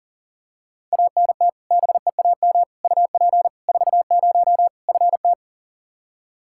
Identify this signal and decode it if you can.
Morse code, heard as a keyed tone, about 40 wpm, 700 Hz. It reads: ANT BEAM UP 40 FT
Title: Practice QSO - Exchange 2 - Info - 40wpm